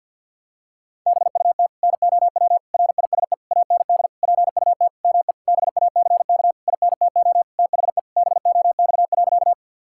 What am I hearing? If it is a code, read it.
Morse code, heard as a keyed tone, about 52 wpm, 700 Hz. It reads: BUT NOW RISE AND PUT ME BACK INTO THE BOX.
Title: Sentences from Top 1000 Words - Review Part 6 - 50wpm